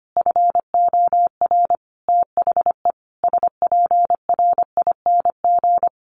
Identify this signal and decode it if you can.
Morse code, heard as a keyed tone, about 25 wpm, 700 Hz. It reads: FOR THE SPRING
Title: Sentences from Top 1200 Words 25wpm